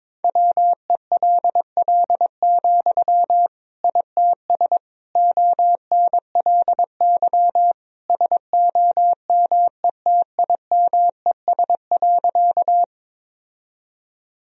Morse code, heard as a keyed tone, about 22 wpm, 700 Hz. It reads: WELL, ITS ONLY SOMETIMES.
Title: Sentences from Top 1000 Words - Review Part 3 - 22wpm